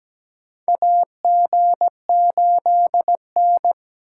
Morse code, heard as a keyed tone, about 17 wpm, 700 Hz. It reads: AG8N
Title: USA Call Signs - 4x Character Spacing - 17wpm